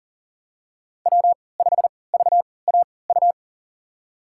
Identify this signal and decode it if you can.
Morse code, heard as a keyed tone, about 39 wpm, spaced as for 23 wpm, 700 Hz. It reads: W5VAU
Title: USA Call Signs - 5 Characters - 3x Character Spacing - 40wpm